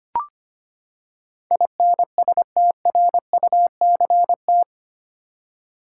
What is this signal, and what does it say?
Morse code, heard as a keyed tone, about 25 wpm, 700 Hz. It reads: INSTRUCT
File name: Eight Letter Words - 25wpm